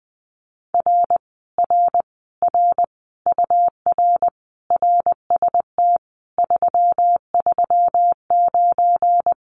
Morse code, heard as a keyed tone, about 20 wpm, 700 Hz. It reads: R R R UR RST 339
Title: Common QSO Phrases Encore 20wpm